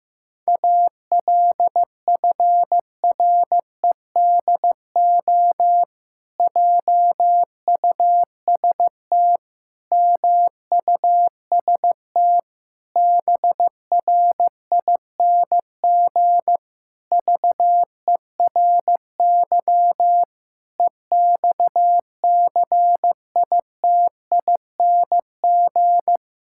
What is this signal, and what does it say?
Morse code, heard as a keyed tone, about 15 wpm, 700 Hz. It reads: ALFREDO JUST MUST BRING VERY EXCITING
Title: Pangram Challenge v2 - 15wpm